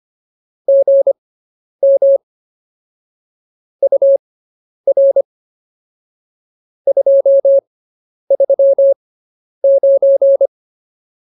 Morse code, heard as a keyed tone, about 24 wpm, spaced as for 10 wpm, 550 Hz. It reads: GM UR 239